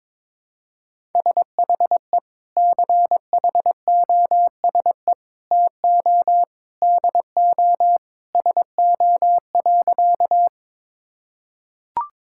Morse code, heard as a keyed tone, about 22 wpm, 700 Hz. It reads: SHE CHOSE TO DO SO.
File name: practice / 22wpm / Sentences with 3-Word Phrases - 22wpm